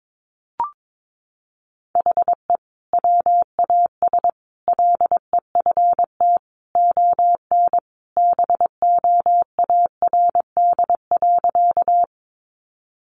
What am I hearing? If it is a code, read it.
Morse code, heard as a keyed tone, about 22 wpm, 700 Hz. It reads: HE WAS LEFT ON BOARD.
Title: Sentences from Top 700 Words 22wpm